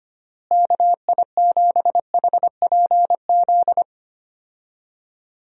Morse code, heard as a keyed tone, about 25 wpm, 700 Hz. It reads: KI7HPZ
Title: USA Call Signs - 6 Characters - Repeated 2x - 25wpm